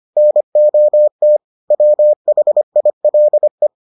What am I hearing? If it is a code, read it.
Morse code, heard as a keyed tone, about 25 wpm, 600 Hz. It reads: NOT WHILE